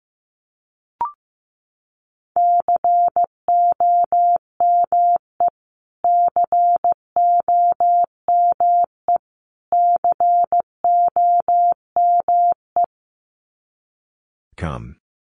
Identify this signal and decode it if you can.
Morse code, heard as a keyed tone, about 15 wpm, 700 Hz. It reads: COME COME COME
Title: Top 500 Words Repeated 15wpm